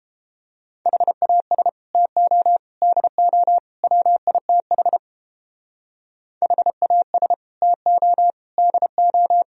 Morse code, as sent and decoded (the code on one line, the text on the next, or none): .... .- ... / - --- / -.. --- / .-- .. - .... / .... .- ... / - --- / -.. ---
HAS TO DO WITH HAS TO DO